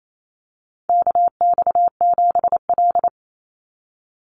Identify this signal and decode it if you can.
Morse code, heard as a keyed tone, about 28 wpm, 700 Hz. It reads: KX7L
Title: NA SOTA Chaser Callsigns 28wpm